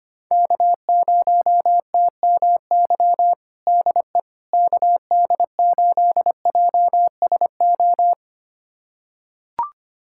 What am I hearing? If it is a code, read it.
Morse code, heard as a keyed tone, about 25 wpm, 700 Hz. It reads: K0TMY DE KD8JSO
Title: USA Callsign DE Callsign 25wpm